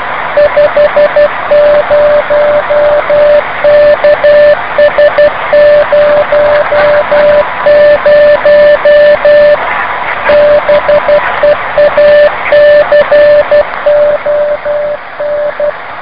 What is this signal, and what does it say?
Morse code, heard as a keyed tone, about 12 wpm, 590 Hz, with noise. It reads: 50KS00 BEACON